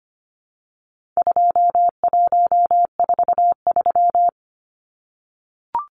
Morse code, heard as a keyed tone, about 25 wpm, 700 Hz. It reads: 2143
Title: Numbers - 4-Digit 25wpm